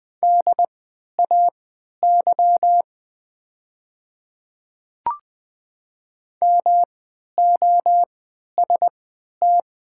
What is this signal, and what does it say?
Morse code, heard as a keyed tone, about 20 wpm, spaced as for 11 wpm, 700 Hz. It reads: DAY MOST
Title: 3x Character Spacing - Top 100 Words - 20wpm